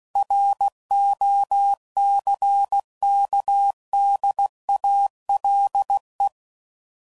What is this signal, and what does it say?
Morse code, heard as a keyed tone, about 16 wpm, 800 Hz. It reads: ROCKDALE
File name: CW Code Groups @15WPM / fromSM's / Cities